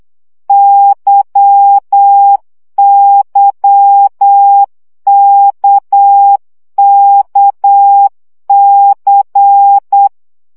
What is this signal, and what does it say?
Morse code, heard as a keyed tone, about 8 wpm, 800 Hz. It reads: YYKKC
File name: CW gyakorlas k c y 7 baud